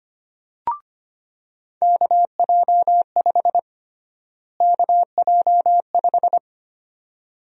Morse code, heard as a keyed tone, about 25 wpm, 700 Hz. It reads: KJ5 KJ5
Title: USA Call Sign Prefixes - Repeated 2x - 25wpm